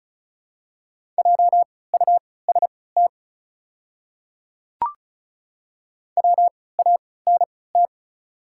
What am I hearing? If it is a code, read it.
Morse code, heard as a keyed tone, about 36 wpm, spaced as for 20 wpm, 700 Hz. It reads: JUST WANT
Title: 3x Character Spacing - Top 200 Words - 35wpm